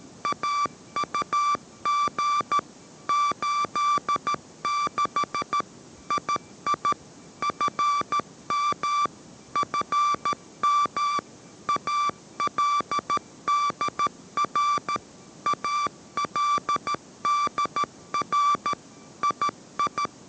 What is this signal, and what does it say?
Morse code, heard as a keyed tone, about 13 wpm, 1170 Hz. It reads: AUG 86 II FM FM ALDR ALDR II